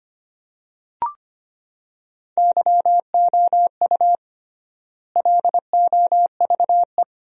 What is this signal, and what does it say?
Morse code, heard as a keyed tone, about 25 wpm, 700 Hz. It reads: YOU LOVE
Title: 3x Word-Spacing - Sentences from Top 400 Words - 25wpm